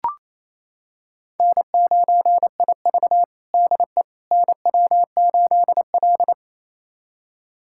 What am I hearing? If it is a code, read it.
Morse code, heard as a keyed tone, about 28 wpm, 700 Hz. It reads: N9IV DE NW8L
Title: USA Callsign DE Callsign - 4 Characters - 28wpm